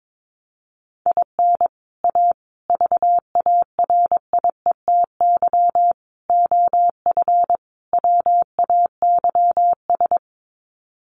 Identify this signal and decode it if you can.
Morse code, heard as a keyed tone, about 22 wpm, 700 Hz. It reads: IN A VARIETY OF WAYS
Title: Sets of 5 Words Encore 22wpm